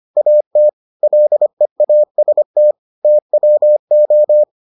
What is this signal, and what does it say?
Morse code, heard as a keyed tone, about 25 wpm, 600 Hz. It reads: AT LEAST TWO